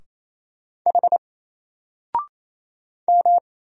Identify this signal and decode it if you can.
Morse code, heard as a keyed tone, about 27 wpm, 700 Hz. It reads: H M
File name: Single-Letter-Number-28wpm